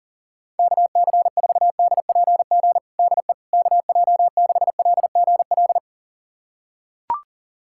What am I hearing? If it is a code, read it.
Morse code, heard as a keyed tone, about 40 wpm, 700 Hz. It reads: KC4DPG DE KJ6LGL
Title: USA Callsign DE Callsign - 6 Characters - 40wpm